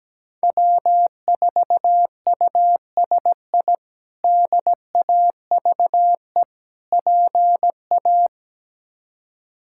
Morse code, heard as a keyed tone, about 17 wpm, 700 Hz. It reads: W4USI DAVE PA
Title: K1USN SST QSO Practice - 17wpm